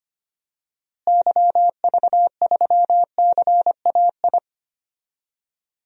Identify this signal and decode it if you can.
Morse code, heard as a keyed tone, about 25 wpm, 700 Hz. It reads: YV3CAI
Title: DXCC-100 Call Signs - 6 Characters - Repeated 3x - 25wpm